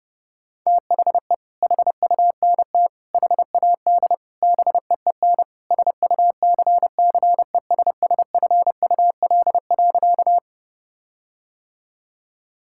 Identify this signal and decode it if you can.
Morse code, heard as a keyed tone, about 30 wpm, 700 Hz. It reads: THE HUNT HAD BEEN SUCCESSFUL.
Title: Sentences from Top 1600 Words 30wpm